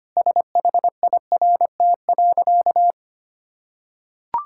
Morse code, heard as a keyed tone, about 25 wpm, 700 Hz. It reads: SHIRT.
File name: Easy Sentences with Hints - 25wpm